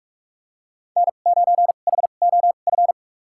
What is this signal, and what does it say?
Morse code, heard as a keyed tone, about 43 wpm, spaced as for 32 wpm, 700 Hz. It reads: N9HOF
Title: USA Call Signs - 2x Character Spacing - 45wpm